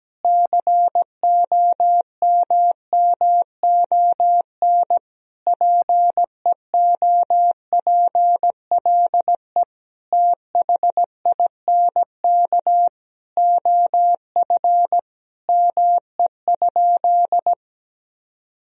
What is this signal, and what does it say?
Morse code, heard as a keyed tone, about 17 wpm, 700 Hz. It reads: COMMON PEOPLE THINK OF ME?
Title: Sentences from Top 1000 Words - Review Part 3 - 17wpm